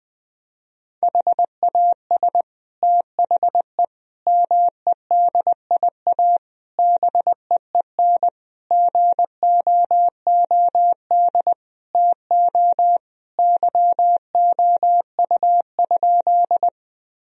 Morse code, heard as a keyed tone, about 20 wpm, 700 Hz. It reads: HAS THE MEDIA BEEN GOOD TO YOU?
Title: Sentences from Top 1000 Words - Review Part 4 - 20wpm